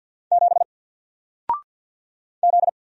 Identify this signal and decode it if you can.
Morse code, heard as a keyed tone, about 54 wpm, 700 Hz. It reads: 7 Z
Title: Single Letter-Number Alt 50wpm